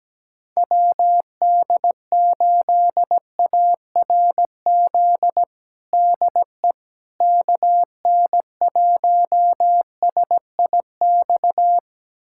Morse code, heard as a keyed tone, about 17 wpm, 700 Hz. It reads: WD8ARZ DE KN1SIX
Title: USA Callsign DE Callsign - 6 Characters - 17wpm